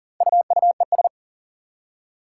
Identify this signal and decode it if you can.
Morse code, heard as a keyed tone, about 39 wpm, 700 Hz. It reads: UUES
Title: Four Challenging Letter-Number Combinations 40wpm